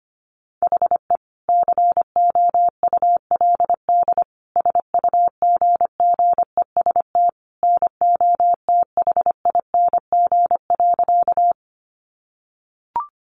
Morse code, heard as a keyed tone, about 25 wpm, 700 Hz. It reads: HE COULD SUGGEST NOTHING.
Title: Sentences from Top 1300 Words 25wpm